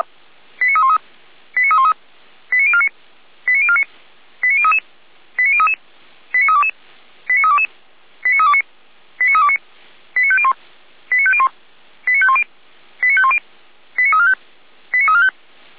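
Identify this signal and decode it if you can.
Five-tone selective call (ZVEI), heard as quick runs of five tones, tones of about 70 ms. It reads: 89313 89313 89049 89049 89033 89033 89322 89322 89329 89329 89501 89501 89510 89510 89345 89345
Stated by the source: ZVEI five-tone call received by SDR, recorded for Feuerwehr Orlamünde.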